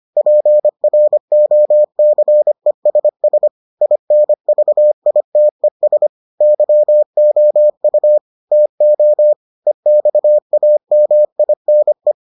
Morse code, heard as a keyed tone, about 25 wpm, 600 Hz. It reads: PROCESS INVITES YOU TO EXAMINE